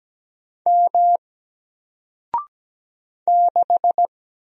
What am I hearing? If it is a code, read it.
Morse code, heard as a keyed tone, about 17 wpm, 700 Hz. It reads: M 6
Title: Single Character (Letter:Number:Punctuation:Prosign) Alt 17wpm